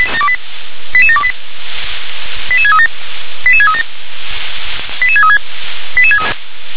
Five-tone selective call (ZVEI), heard as quick runs of five tones, tones of about 70 ms. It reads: 88728 88637 88637 88636